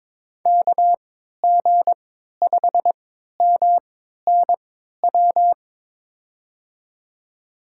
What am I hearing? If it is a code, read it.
Morse code, heard as a keyed tone, about 22 wpm, spaced as for 12 wpm, 700 Hz. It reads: KG5MNW
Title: USA Call Signs - 6 Characters - 3x Character Spacing - 22wpm